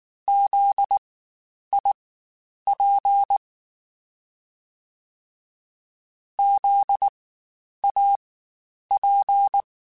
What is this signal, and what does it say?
Morse code, heard as a keyed tone, about 19 wpm, spaced as for 9 wpm, 800 Hz. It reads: ZIP ZAP
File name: Spaced Alphas / Spaced Alphas mp3s / Zs sp @18